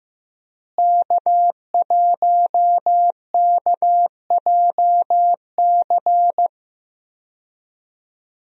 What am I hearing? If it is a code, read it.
Morse code, heard as a keyed tone, about 15 wpm, 700 Hz. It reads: K1KJC
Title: USA Call Signs - 5 Characters - Repeated 3x - 15wpm